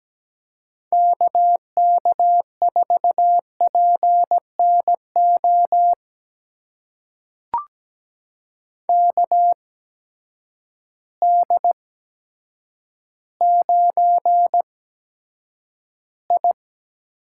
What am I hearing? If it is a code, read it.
Morse code, heard as a keyed tone, about 17 wpm, spaced as for 8 wpm, 700 Hz. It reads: KK4PNO KD9I
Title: USA Call Signs - 6 Characters - 8x Character Spacing - 17wpm